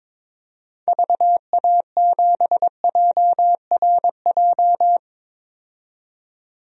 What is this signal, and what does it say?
Morse code, heard as a keyed tone, about 22 wpm, 700 Hz. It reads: VA7JRJ
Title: DXCC-100 Call Signs - 6 Characters - 22wpm